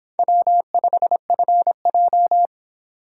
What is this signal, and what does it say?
Morse code, heard as a keyed tone, about 29 wpm, 700 Hz. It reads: W5FJ